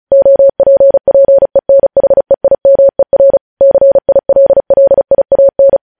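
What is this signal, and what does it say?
Morse code, heard as a keyed tone, about 36 wpm, 550 Hz. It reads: OPPENHEIMER CILLIAN
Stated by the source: Morse code transmission of news headlines, preceded by vvv.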